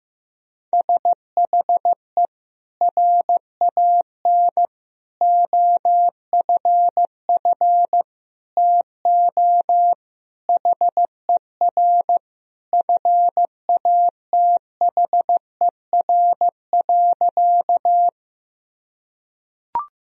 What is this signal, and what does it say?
Morse code, heard as a keyed tone, about 15 wpm, 700 Hz. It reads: SHE RAN OFF TO HER FATHER.
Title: Sentences from Top 1000 Words - Review Part 3 - 15wpm